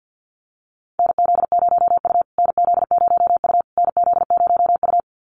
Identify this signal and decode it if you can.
Morse code, heard as a keyed tone, about 50 wpm, 700 Hz. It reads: NZ0U NZ0U NZ0U
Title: USA Call Signs Repeated 3x 50wpm